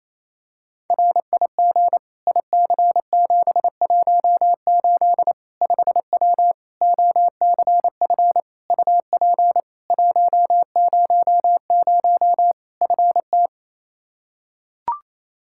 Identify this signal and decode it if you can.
Morse code, heard as a keyed tone, about 28 wpm, 700 Hz. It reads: RIG IC718 5W OCF UP 100 FT
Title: K1USN SST QSO Practice 28wpm